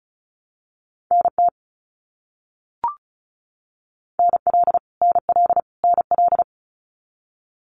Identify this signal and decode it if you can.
Morse code, heard as a keyed tone, about 34 wpm, 700 Hz. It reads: NT NL NL NL
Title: Canadian Provinces and Territories - Repeated 3x - 35wpm